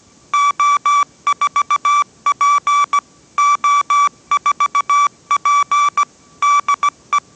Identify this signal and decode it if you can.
Morse code, heard as a keyed tone, about 17 wpm, 1160 Hz. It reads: O4P O4P DE